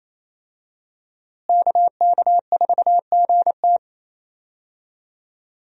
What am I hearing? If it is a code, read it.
Morse code, heard as a keyed tone, about 28 wpm, 700 Hz. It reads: KK4GT
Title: USA Call Signs - 5 Characters - Repeated 2x - 28wpm